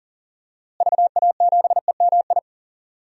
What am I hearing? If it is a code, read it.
Morse code, heard as a keyed tone, about 40 wpm, 700 Hz. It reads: VA7EMI
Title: DXCC-100 Call Signs - 6 Characters - Repeated 2x - 40wpm